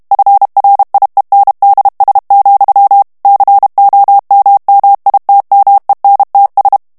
Morse code, heard as a keyed tone, about 32 wpm, 800 Hz. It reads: FRIENDS, COMMITMENTS
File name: morse mp3 / ANGLAIS texte clair 1 vitesse 30 mots minute